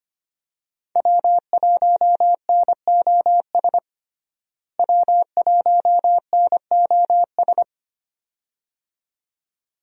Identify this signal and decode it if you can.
Morse code, heard as a keyed tone, about 25 wpm, 700 Hz. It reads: W1NOS W1NOS
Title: USA Call Signs - 5 Characters - Repeated 2x - 25wpm